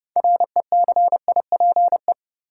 Morse code, heard as a keyed tone, about 30 wpm, 700 Hz. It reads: RECIPE
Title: Sentences from Top 1200 Words - New Word Drill Alt - 30wpm